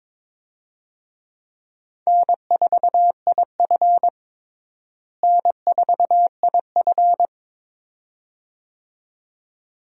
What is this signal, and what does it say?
Morse code, heard as a keyed tone, about 22 wpm, 700 Hz. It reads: N4IF N4IF
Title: USA Call Signs - 4 Characters - Repeated 2x - 22wpm